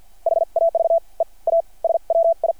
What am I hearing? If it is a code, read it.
Morse code, heard as a keyed tone, about 50 wpm, spaced as for 38 wpm, 660 Hz. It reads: HAVE A SWI
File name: archived / CRSnet